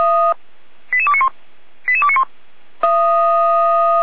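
Five-tone selective call (ZVEI), heard as quick runs of five tones, tones of about 70 ms. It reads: 88291 88291
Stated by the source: ZVEI five-tone code 88335 received by SDR, from the feuerwehr-orlamuende.org (fire brigade) audio server.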